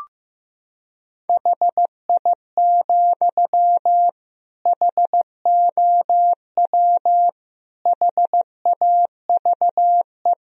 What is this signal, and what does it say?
Morse code, heard as a keyed tone, about 16 wpm, 700 Hz. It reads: HI, HOW HAVE
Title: Sentences from Top 1000 Words - Review Part 3 - 15wpm